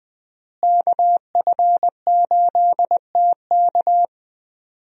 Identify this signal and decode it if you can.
Morse code, heard as a keyed tone, about 20 wpm, 700 Hz. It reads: KF8TK
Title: USA Call Signs - 5 Characters - Repeated 3x - 20wpm